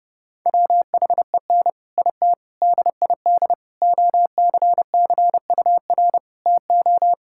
Morse code, heard as a keyed tone, about 32 wpm, 700 Hz. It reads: WHEN IT DID OCCUR TO
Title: Sentences from Top 1000 Words - Review Part 6 - 30wpm